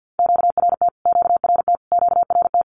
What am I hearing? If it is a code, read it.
Morse code, heard as a keyed tone, about 50 wpm, 700 Hz. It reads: QRT QRT QRT
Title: Q-Codes Repeated 3x 50wpm